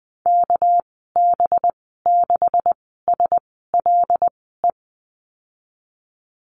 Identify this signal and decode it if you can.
Morse code, heard as a keyed tone, about 19 wpm, spaced as for 14 wpm, 700 Hz. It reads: KB6SLE
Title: 200 Lesson 32g - Letter k - Call Signs 20wpm